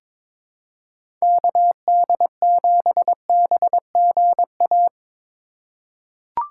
Morse code, heard as a keyed tone, about 21 wpm, 700 Hz. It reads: KD7BGA
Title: USA Call Signs - 6 Characters - 3x Character Spacing - 22wpm